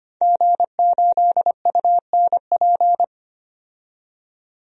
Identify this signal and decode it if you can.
Morse code, heard as a keyed tone, about 25 wpm, 700 Hz. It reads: G8UNP
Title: DXCC-100 Call Signs - 5 Characters - 25wpm